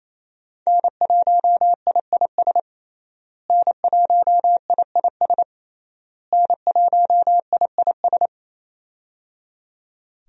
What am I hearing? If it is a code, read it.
Morse code, heard as a keyed tone, about 27 wpm, 700 Hz. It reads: N1IIS N1IIS N1IIS
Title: USA Call Signs - 5 Characters - Repeated 3x - 28wpm